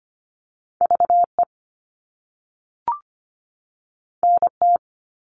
Morse code, heard as a keyed tone, about 25 wpm, 700 Hz. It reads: V E N T
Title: All Cut Numbers - 2-Digit 25wpm